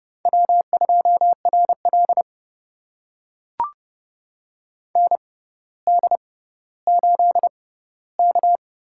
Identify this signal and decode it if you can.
Morse code, heard as a keyed tone, about 30 wpm, spaced as for 15 wpm, 700 Hz. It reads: W2RL ND8K
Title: USA Call Signs - 4 Characters - 6x Character Spacing - 30wpm